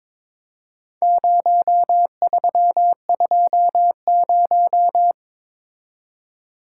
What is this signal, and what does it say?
Morse code, heard as a keyed tone, about 22 wpm, 700 Hz. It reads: 0320
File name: practice / 22wpm / Four Letter-Number Combinations - 22wpm